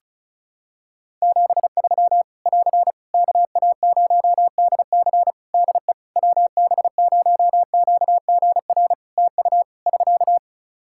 Morse code, heard as a keyed tone, about 34 wpm, 700 Hz. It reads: 73 <AR> KA0DC DE WB0QGR TU <SK>
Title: Practice QSO - Exchange 4 - Ending - 35wpm